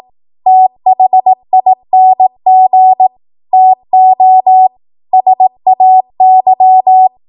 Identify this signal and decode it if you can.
Morse code, heard as a keyed tone, about 18 wpm, 750 Hz. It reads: THING TO SAY
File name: Morse / 18WPM